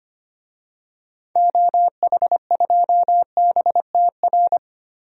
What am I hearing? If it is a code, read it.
Morse code, heard as a keyed tone, about 25 wpm, 700 Hz. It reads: OH2BTR
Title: DXCC-100 Call Signs - 6 Characters - 25wpm